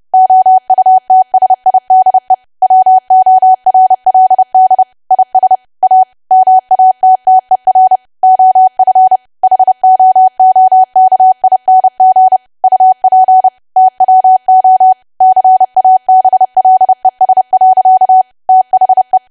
Morse code, heard as a keyed tone, about 30 wpm, 750 Hz. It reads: OUTSIDE WORLD IS A MATTER OF HOOKING UP TWO CABLES. THE
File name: cwa / W1AW-Level3 / 30WPM